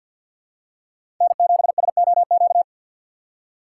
Morse code, heard as a keyed tone, about 50 wpm, 700 Hz. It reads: N7SOQ
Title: USA Call Signs - 5 Characters - 3x Character Spacing - 50wpm